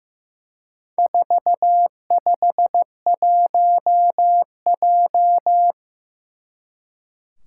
Morse code, heard as a keyed tone, about 15 wpm, 700 Hz. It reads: 451J
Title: Four Challenging Letter-Number Combinations 15wpm